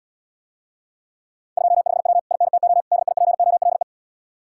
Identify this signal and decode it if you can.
Morse code, heard as a keyed tone, about 80 wpm, 700 Hz. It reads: 25W INTO DIPOLE